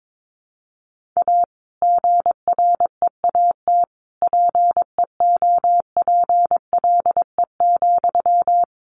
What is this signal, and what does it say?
Morse code, heard as a keyed tone, about 22 wpm, 700 Hz. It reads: A GREAT PEOPLE,
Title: Sentences from Top 1500 Words 22wpm